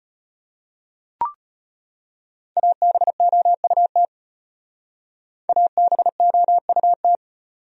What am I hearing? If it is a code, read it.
Morse code, heard as a keyed tone, about 32 wpm, 700 Hz. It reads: ABOUT ABOUT